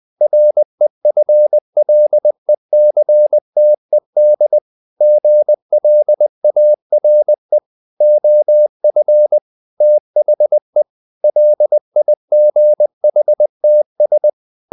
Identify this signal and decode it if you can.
Morse code, heard as a keyed tone, about 20 wpm, 600 Hz. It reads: REFLECTED GLARE OF THE LIGHTS
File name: the old man and the sea part 2 20wpm